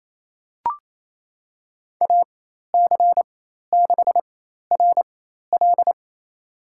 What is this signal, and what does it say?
Morse code, heard as a keyed tone, about 28 wpm, spaced as for 13 wpm, 700 Hz. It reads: AC6RL
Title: USA Call Signs - 5 Characters - 4x Character Spacing - 28wpm